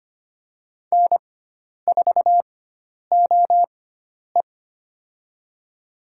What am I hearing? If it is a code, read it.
Morse code, heard as a keyed tone, about 25 wpm, spaced as for 10 wpm, 700 Hz. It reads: N4OE
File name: Lesson 16g - Letter c - Call Signs 25wpm